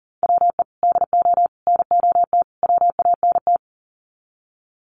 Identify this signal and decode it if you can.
Morse code, heard as a keyed tone, about 40 wpm, 700 Hz. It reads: WE DO NOT WANT
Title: Sets of 4 Words 40wpm